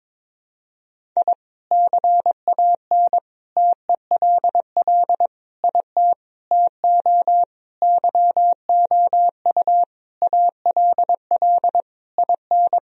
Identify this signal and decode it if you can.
Morse code, heard as a keyed tone, about 22 wpm, 700 Hz. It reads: I CAN TELL IT TO YOU ALL IN
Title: Sentences from Top 1000 Words - Review Part 1 - 22wpm